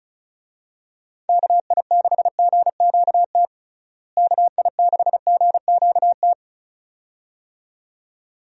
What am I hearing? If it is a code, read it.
Morse code, heard as a keyed tone, about 36 wpm, 700 Hz. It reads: KI6GQT KI6GQT
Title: USA Call Signs - 6 Characters - Repeated 2x - 35wpm